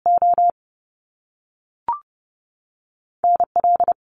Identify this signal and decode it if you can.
Morse code, heard as a keyed tone, about 30 wpm, 700 Hz. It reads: O NL
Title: Top Two Letter Combinations in Words 30wpm